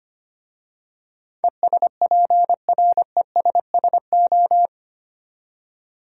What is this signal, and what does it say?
Morse code, heard as a keyed tone, about 25 wpm, 700 Hz. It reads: ESPRESSO
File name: practice / 25wpm / Easy Words with Hints - 25wpm